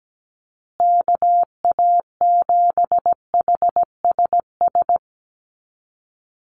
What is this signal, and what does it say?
Morse code, heard as a keyed tone, about 17 wpm, 700 Hz. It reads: KA7HSS
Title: USA Call Signs Repeated 2x 17wpm